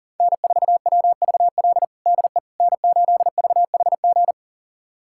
Morse code, heard as a keyed tone, about 40 wpm, 700 Hz. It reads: N4WVL DE N8VHG
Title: USA Callsign DE Callsign - 5 Characters - 40wpm